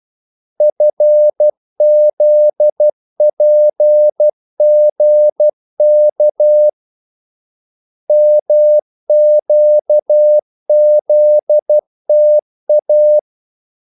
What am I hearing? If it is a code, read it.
Morse code, heard as a keyed tone, about 12 wpm, 600 Hz. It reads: FZPGK MQZTA
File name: code-exemple